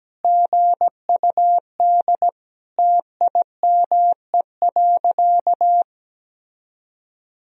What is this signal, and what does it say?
Morse code, heard as a keyed tone, about 17 wpm, 700 Hz. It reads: GUD TIME.
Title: Sentences from Top 200 Words and Abbreviations 17wpm